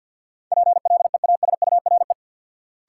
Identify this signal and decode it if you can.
Morse code, heard as a keyed tone, about 47 wpm, 700 Hz. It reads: PLEASURE
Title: Sentences from Top 1600 Words - New Word Drill Alt - 50wpm